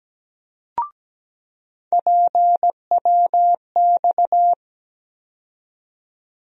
Morse code, heard as a keyed tone, about 17 wpm, 700 Hz. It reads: PWX
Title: USA Call Sign Postfixes - 3 Letters 17wpm